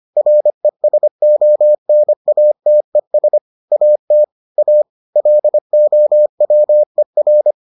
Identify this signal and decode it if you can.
Morse code, heard as a keyed tone, about 25 wpm, 600 Hz. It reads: RESONATES AT A LOWER